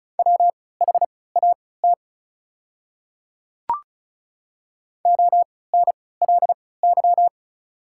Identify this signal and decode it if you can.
Morse code, heard as a keyed tone, about 36 wpm, spaced as for 20 wpm, 700 Hz. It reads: WHAT ONLY
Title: 3x Character Spacing - Top 200 Words - 35wpm